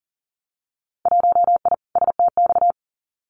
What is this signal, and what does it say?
Morse code, heard as a keyed tone, about 40 wpm, 700 Hz. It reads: 1I STX
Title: Winter Field Day Exchanges 40wpm